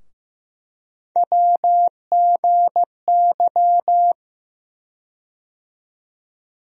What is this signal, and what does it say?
Morse code, heard as a keyed tone, about 15 wpm, 700 Hz. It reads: WGY
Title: USA Call Sign Postfixes - 3 Letters 15wpm